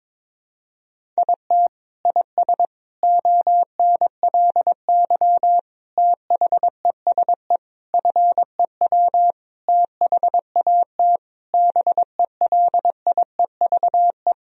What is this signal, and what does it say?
Morse code, heard as a keyed tone, about 22 wpm, 700 Hz. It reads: IT IS ONLY THESE FEW THAT BELIEVE
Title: Sentences from Top 1000 Words - Review Part 1 - 22wpm